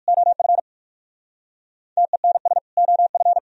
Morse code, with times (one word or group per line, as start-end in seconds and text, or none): 0.1-0.6 OF
2.0-2.6 TENS
2.8-3.4 OF